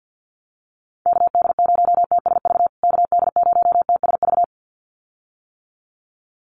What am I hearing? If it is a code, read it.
Morse code, heard as a keyed tone, about 50 wpm, 700 Hz. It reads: KD0TSV KD0TSV
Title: USA Call Signs Repeated 2x 50wpm